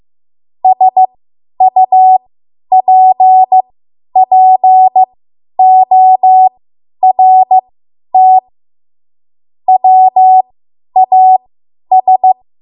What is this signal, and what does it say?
Morse code, heard as a keyed tone, about 15 wpm, spaced as for 10 wpm, 750 Hz. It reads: SUPPORT WAS
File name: Morse / 10WPM